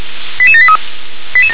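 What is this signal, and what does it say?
Five-tone selective call (ZVEI), heard as quick runs of five tones, tones of about 80 ms. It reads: 88783
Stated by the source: ZVEI five-tone call received with an SDR.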